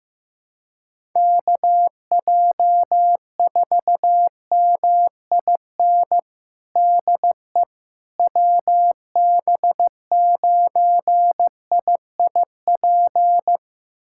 Morse code, heard as a keyed tone, about 16 wpm, 700 Hz. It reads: KJ4MIN DE WB9IIP
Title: USA Callsign DE Callsign - 6 Characters - 15wpm